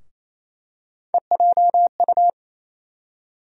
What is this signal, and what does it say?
Morse code, heard as a keyed tone, about 28 wpm, 700 Hz. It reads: EJU
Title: USA Call Sign Postfixes - 3 Letters 28wpm